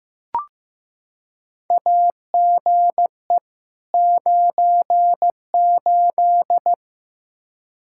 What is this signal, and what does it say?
Morse code, heard as a keyed tone, about 16 wpm, 700 Hz. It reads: AGE 98
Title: Practice QSO - Exchange 3 - More Info - 15wpm